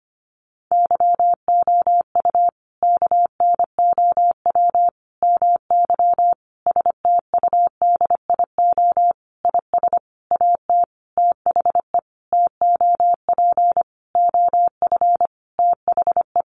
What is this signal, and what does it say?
Morse code, heard as a keyed tone, about 25 wpm, 700 Hz. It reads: YOU KNOW MY STUDIO IS AT THE TOP OF THE
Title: Sentences from Top 1200 Words 25wpm